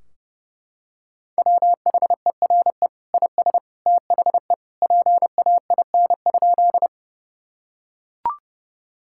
Morse code, heard as a keyed tone, about 28 wpm, 700 Hz. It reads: WHERE IS THE PAIN?
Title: Sentences from Top 1600 Words 30wpm